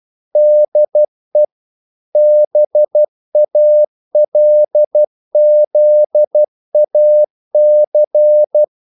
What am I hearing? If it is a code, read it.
Morse code, heard as a keyed tone, about 12 wpm, 600 Hz. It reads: DE BALZAC
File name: CW / Practice / EnglishQuotations0312